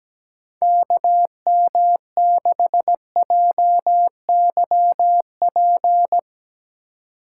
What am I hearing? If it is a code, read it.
Morse code, heard as a keyed tone, about 17 wpm, 700 Hz. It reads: KM6JYP
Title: USA Call Signs - 6 Characters - 2x Character Spacing - 17wpm